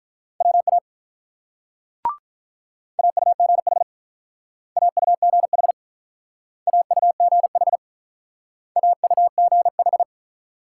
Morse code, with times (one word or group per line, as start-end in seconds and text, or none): 0.4-0.8 WA
3.0-3.8 AUGH
4.8-5.7 AUGH
6.7-7.8 AUGH
8.8-10.0 AUGH